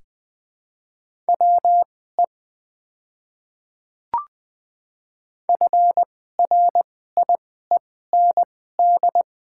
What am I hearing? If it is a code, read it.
Morse code, heard as a keyed tone, about 20 wpm, spaced as for 14 wpm, 700 Hz. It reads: WE FRIEND
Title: 2x Character Spacing - Top 500 Words - 20wpm